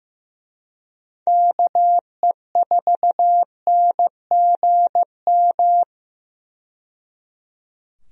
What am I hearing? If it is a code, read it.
Morse code, heard as a keyed tone, about 15 wpm, 700 Hz. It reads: KE4NGM
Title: USA Call Signs - 6 Characters - 15wpm